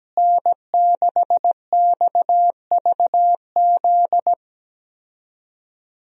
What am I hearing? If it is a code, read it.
Morse code, heard as a keyed tone, about 17 wpm, 700 Hz. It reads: N6XVZ
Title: USA Call Signs - 5 Characters - Repeated 3x - 17wpm